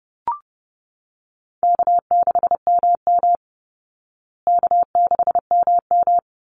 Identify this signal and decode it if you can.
Morse code, heard as a keyed tone, about 30 wpm, 700 Hz. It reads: K6MM K6MM
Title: Call Signs Repeated 3x - 30wpm